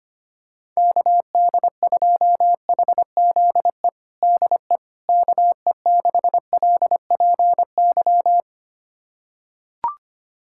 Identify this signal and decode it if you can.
Morse code, heard as a keyed tone, about 25 wpm, 700 Hz. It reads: KD2HZE DE KE6LPY
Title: USA Callsign DE Callsign - 6 Characters - 25wpm